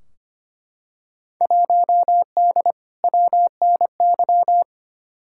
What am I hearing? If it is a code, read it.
Morse code, heard as a keyed tone, about 25 wpm, 700 Hz. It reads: 1D WNY
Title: Field Day Exchanges - 25wpm